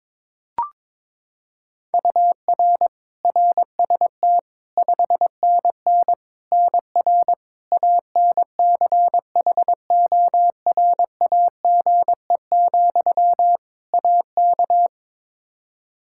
Morse code, heard as a keyed tone, about 22 wpm, 700 Hz. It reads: UR RST 5NN NR ANCHORAGE, AK
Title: Practice QSO - Exchange 1 - Beginning - 22wpm